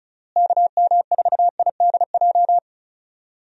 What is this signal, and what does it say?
Morse code, heard as a keyed tone, about 34 wpm, 700 Hz. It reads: KM4IDJ
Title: USA Call Signs - 6 Characters - Repeated 3x - 35wpm